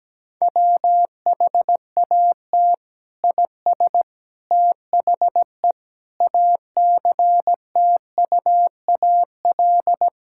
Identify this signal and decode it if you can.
Morse code, heard as a keyed tone, about 17 wpm, 700 Hz. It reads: WHAT IS THE ACTUAL
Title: Sentences from Top 1000 Words - Review Part 5 - 17wpm